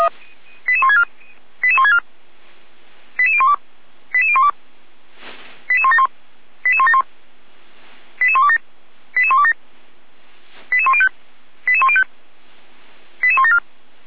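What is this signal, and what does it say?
Five-tone selective call (ZVEI), heard as quick runs of five tones, tones of about 70 ms. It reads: 88164 88164 88012 88012 88171 88171 88127 88127 88105 88105 88174
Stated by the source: ZVEI five-tone call 88122 received on an SDR.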